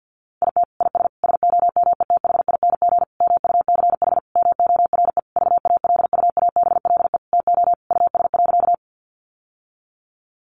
Morse code, heard as a keyed tone, about 50 wpm, 700 Hz. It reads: IT IS SOMETHING MUCH MORE VALUABLE TO US.
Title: Sentences from Top 1100 Words 50wpm